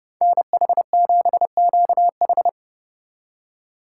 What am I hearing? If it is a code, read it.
Morse code, heard as a keyed tone, about 35 wpm, 700 Hz. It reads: NH7QH
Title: Speed Racing - USA 2x2 Call Signs 30wpm